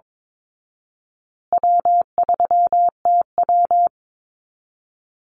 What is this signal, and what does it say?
Morse code, heard as a keyed tone, about 22 wpm, 700 Hz. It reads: W3TW
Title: Easy Call Signs Repeated 3x - 22wpm